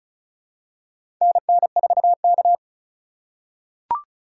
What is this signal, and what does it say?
Morse code, heard as a keyed tone, about 36 wpm, 700 Hz. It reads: NN4K
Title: USA Call Signs - 4 Characters - 4x Character Spacing - 35wpm